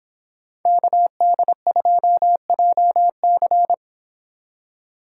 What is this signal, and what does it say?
Morse code, heard as a keyed tone, about 24 wpm, 700 Hz. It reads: KD2JC